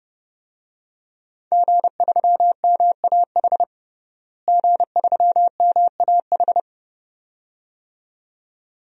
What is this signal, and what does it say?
Morse code, heard as a keyed tone, about 30 wpm, 700 Hz. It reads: G3MAH G3MAH
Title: DXCC-100 Call Signs - 5 Characters - Repeated 2x - 30wpm